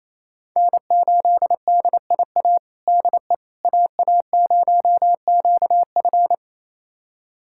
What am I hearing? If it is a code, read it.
Morse code, heard as a keyed tone, about 28 wpm, 700 Hz. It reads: N8DIA DE AA0QF
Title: USA Callsign DE Callsign - 5 Characters - 28wpm